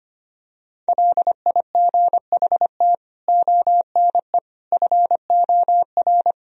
Morse code, heard as a keyed tone, about 25 wpm, 700 Hz. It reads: LIGHT ONE FOR
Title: Sentences from Top 1000 Words - Review Part 1 - 25wpm